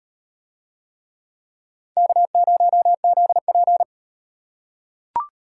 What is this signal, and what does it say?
Morse code, heard as a keyed tone, about 36 wpm, 700 Hz. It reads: K0ZP